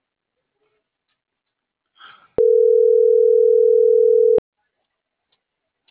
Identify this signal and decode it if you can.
Ringback tone (North America), 440 + 480 Hz.